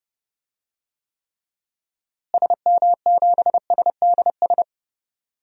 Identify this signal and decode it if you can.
Morse code, heard as a keyed tone, about 30 wpm, 700 Hz. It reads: SM7SDS
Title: DXCC-100 Call Signs - 6 Characters - Repeated 3x - 30wpm